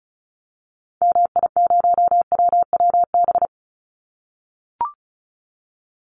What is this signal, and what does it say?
Morse code, heard as a keyed tone, about 34 wpm, 700 Hz. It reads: MI0WWB
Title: Call Signs - 35wpm